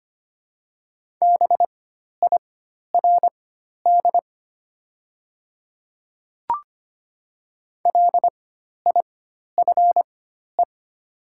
Morse code, heard as a keyed tone, about 25 wpm, spaced as for 12 wpm, 700 Hz. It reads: BIRD LIFE
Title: 4x Character Spacing - Top 500 Words - 25wpm